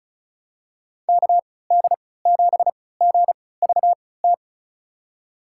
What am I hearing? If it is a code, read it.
Morse code, heard as a keyed tone, about 34 wpm, spaced as for 20 wpm, 700 Hz. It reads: KD7GVT
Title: USA Call Signs - 6 Characters - 3x Character Spacing - 35wpm